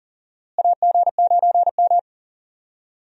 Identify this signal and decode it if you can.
Morse code, heard as a keyed tone, about 38 wpm, 700 Hz. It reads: AG9M
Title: USA Call Signs - 4 Characters - Repeated 2x - 40wpm